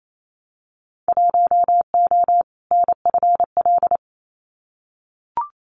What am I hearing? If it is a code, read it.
Morse code, heard as a keyed tone, about 28 wpm, 700 Hz. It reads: 1O NFL
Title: Winter Field Day Exchanges 28wpm